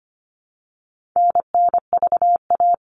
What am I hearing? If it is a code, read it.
Morse code, heard as a keyed tone, about 25 wpm, 700 Hz. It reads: NNVA
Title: All Cut Numbers - 4-Digit 25wpm